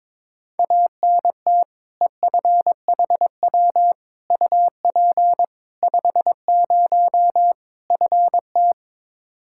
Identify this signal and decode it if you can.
Morse code, heard as a keyed tone, about 22 wpm, 700 Hz. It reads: ANT EFHW UP 50 FT
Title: Practice QSO - Exchange 2 - Info - 22wpm